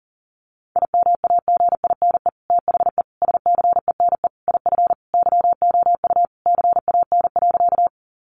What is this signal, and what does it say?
Morse code, heard as a keyed tone, about 40 wpm, 700 Hz. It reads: IMAGINE THE SCENE IF YOU CAN.
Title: Sentences from Top 1400 Words 40wpm